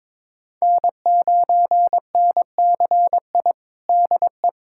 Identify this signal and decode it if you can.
Morse code, heard as a keyed tone, about 23 wpm, 700 Hz. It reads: N9NCI DE
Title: USA Callsign DE Callsign - 5 Characters - 22wpm